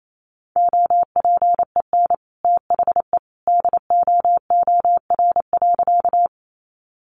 Morse code, heard as a keyed tone, about 28 wpm, 700 Hz. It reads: OPEN THE DOOR.
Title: Sentences from Top 300 Words 28wpm